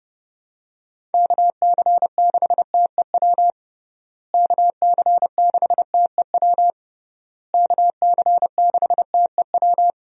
Morse code, heard as a keyed tone, about 30 wpm, 700 Hz. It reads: KC6TEW KC6TEW KC6TEW
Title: USA Call Signs - 6 Characters - Repeated 3x - 30wpm